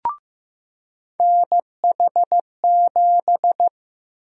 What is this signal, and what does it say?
Morse code, heard as a keyed tone, about 15 wpm, 700 Hz. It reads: NH7
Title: USA Call Sign Prefixes - 15wpm